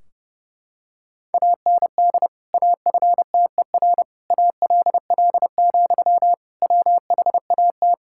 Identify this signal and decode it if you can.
Morse code, heard as a keyed tone, about 30 wpm, 700 Hz. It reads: AND AFTER ALL, WHAT
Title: Sentences from Top 1600 Words 30wpm